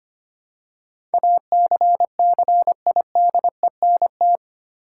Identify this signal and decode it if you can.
Morse code, heard as a keyed tone, about 25 wpm, 700 Hz. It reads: ACCIDENT
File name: 8 Letter Words - First 3 Letters Given - 25wpm